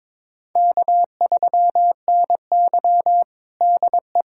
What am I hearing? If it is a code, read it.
Morse code, heard as a keyed tone, about 22 wpm, 700 Hz. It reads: K3NY DE
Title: USA Callsign DE Callsign - 4 Characters - 22wpm